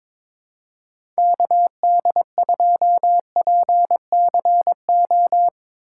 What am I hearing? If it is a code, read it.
Morse code, heard as a keyed tone, about 22 wpm, 700 Hz. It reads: KD2PCO
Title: USA Call Signs - 4x Character Spacing - 22wpm